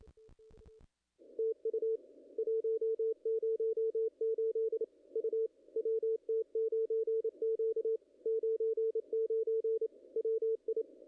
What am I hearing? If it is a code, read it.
Morse code, heard as a keyed tone, about 26 wpm, 440 Hz, with radio noise. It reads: TU 108 U WT9Q 99 WI
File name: contests / ss-cw-k5tr-op-n5rz -0000